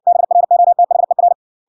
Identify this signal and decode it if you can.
Morse code, heard as a keyed tone, about 56 wpm, 700 Hz. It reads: BROTHER